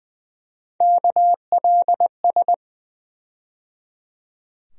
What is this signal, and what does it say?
Morse code, heard as a keyed tone, about 20 wpm, 700 Hz. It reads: KLS
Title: USA Call Sign Postfixes - 3 Letters 20wpm